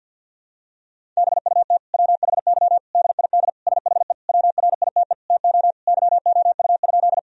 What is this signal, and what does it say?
Morse code, heard as a keyed tone, about 50 wpm, 700 Hz. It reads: BUT WHY DID SHE WRITE TO YOU?
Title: Sentences from Top 1000 Words - Review Part 1 - 50wpm